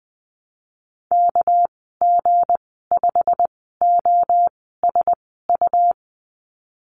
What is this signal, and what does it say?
Morse code, heard as a keyed tone, about 20 wpm, spaced as for 15 wpm, 700 Hz. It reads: KG5OSU
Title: 207 Lesson 33g - Letter j - Call Signs 20wpm